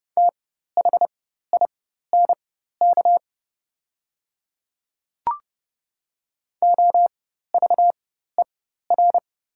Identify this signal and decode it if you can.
Morse code, heard as a keyed tone, about 29 wpm, spaced as for 14 wpm, 690 Hz. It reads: THINK OVER
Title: 4x Character Spacing - Top 200 Words - 30wpm